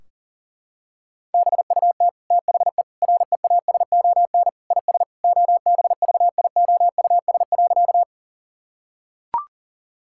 Morse code, heard as a keyed tone, about 40 wpm, 700 Hz. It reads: BUT THE REASON IS OBVIOUS.
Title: Sentences from Top 1600 Words 40wpm